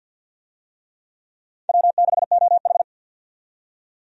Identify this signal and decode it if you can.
Morse code, heard as a keyed tone, about 46 wpm, 700 Hz. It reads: W6OH
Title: USA Call Signs - 4 Characters - 50wpm